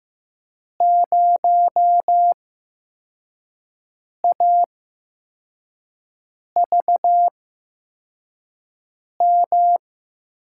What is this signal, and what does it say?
Morse code, heard as a keyed tone, about 15 wpm, spaced as for 4 wpm, 700 Hz. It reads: 0AVM
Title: USA Call Signs - 5 Characters - 8x Character Spacing - 15wpm